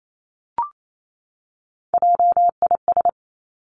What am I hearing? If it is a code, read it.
Morse code, heard as a keyed tone, about 28 wpm, 700 Hz. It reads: JIS
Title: Three Challenging Letter-Number Combinations 28wpm